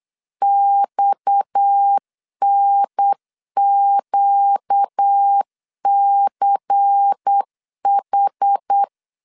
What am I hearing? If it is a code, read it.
Morse code, heard as a keyed tone, about 8 wpm, 800 Hz. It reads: XNQCH
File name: LEZIONI / 6 PKX0 / 6 PKX 8wpm 3041